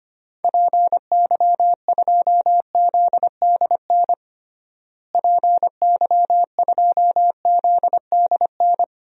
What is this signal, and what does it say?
Morse code, heard as a keyed tone, about 25 wpm, 700 Hz. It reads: PY2ZDN PY2ZDN
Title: DXCC-100 Call Signs - 6 Characters - Repeated 3x - 25wpm